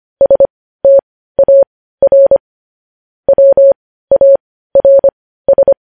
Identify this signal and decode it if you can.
Morse code, heard as a keyed tone, about 25 wpm, spaced as for 15 wpm, 550 Hz. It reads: STAR WARS
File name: podcast / News headlines-1x-25x15